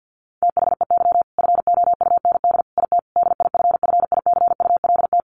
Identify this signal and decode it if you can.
Morse code, heard as a keyed tone, about 50 wpm, 700 Hz. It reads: THEY FOUND IT DIFFICULT
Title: Sentences from Top 1400 Words 50wpm